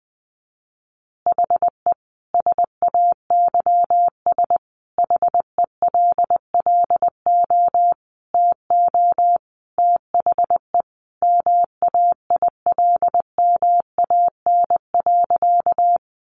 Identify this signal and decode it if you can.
Morse code, heard as a keyed tone, about 20 wpm, 700 Hz. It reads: HE SAYS HELLO TO THE MAILMAN.
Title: Easy Phrases 20wpm